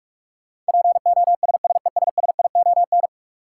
Morse code, heard as a keyed tone, about 46 wpm, 700 Hz. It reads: POSSESSION
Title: Sentences from Top 1400 Words - New Word Drill - 45wpm